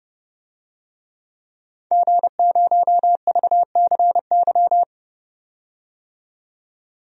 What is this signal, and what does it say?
Morse code, heard as a keyed tone, about 29 wpm, 700 Hz. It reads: G0VCY
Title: DXCC-100 Call Signs - 5 Characters - 30wpm